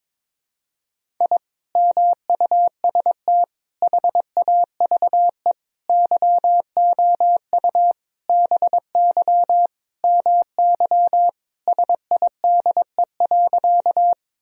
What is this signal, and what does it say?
Morse code, heard as a keyed tone, about 22 wpm, 700 Hz. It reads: I MUST HAVE YOU BY MY SIDE.
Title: Sentences from Top 1000 Words - Review Part 1 - 22wpm